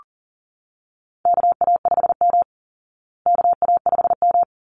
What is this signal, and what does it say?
Morse code, heard as a keyed tone, about 40 wpm, 700 Hz. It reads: KA5M KA5M
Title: Call Signs Repeated 3x - 40wpm